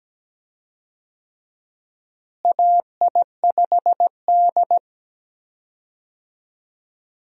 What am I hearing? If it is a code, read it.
Morse code, heard as a keyed tone, about 17 wpm, 700 Hz. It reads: AI5D
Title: USA Call Signs - 4 Characters - Repeated 3x - 17wpm